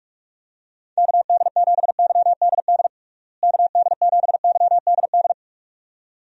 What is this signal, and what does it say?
Morse code, heard as a keyed tone, about 42 wpm, 700 Hz. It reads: KD7YDD KD7YDD
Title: USA Call Signs - 6 Characters - Repeated 2x - 45wpm